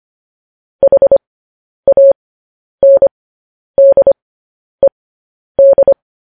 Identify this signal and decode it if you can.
Morse code, heard as a keyed tone, about 25 wpm, spaced as for 10 wpm, 550 Hz. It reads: HANDED
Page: News Headlines in Morse Code at 10 WPM